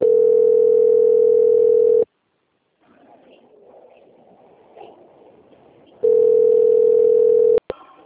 Ringback tone (North America), 440 + 480 Hz.